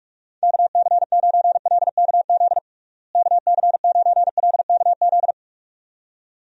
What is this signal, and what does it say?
Morse code, heard as a keyed tone, about 46 wpm, 700 Hz. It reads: KC9LKZ KC9LKZ
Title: USA Call Signs - 6 Characters - Repeated 2x - 45wpm